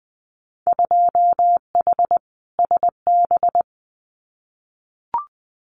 Morse code, heard as a keyed tone, about 20 wpm, 700 Hz. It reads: 2H SB
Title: Winter Field Day Exchanges Encore 20wpm